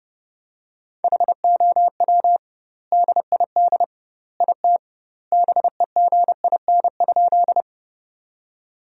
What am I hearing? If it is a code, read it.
Morse code, heard as a keyed tone, about 30 wpm, 700 Hz. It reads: HOW DID IT BEGIN?
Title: 2x Word-Spacing - Sentences from Top 300 Words - 30wpm